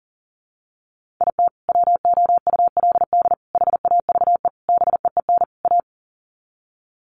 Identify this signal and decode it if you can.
Morse code, heard as a keyed tone, about 40 wpm, 700 Hz. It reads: IT WOULD HAVE BEEN A
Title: Sets of 5 Words Encore 40wpm